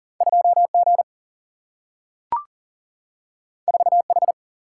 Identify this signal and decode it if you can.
Morse code, heard as a keyed tone, about 40 wpm, 700 Hz. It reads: 2 G 4 H
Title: Single-Letter Single-Number 40wpm